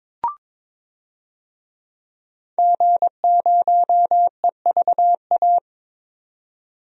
Morse code, heard as a keyed tone, about 22 wpm, 700 Hz. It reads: G0EVA
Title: DXCC-100 Call Signs - 5 Characters - 22wpm